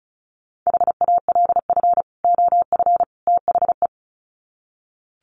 Morse code, heard as a keyed tone, about 34 wpm, 700 Hz. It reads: HALF OF THE
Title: Sets of 3 Words 35wpm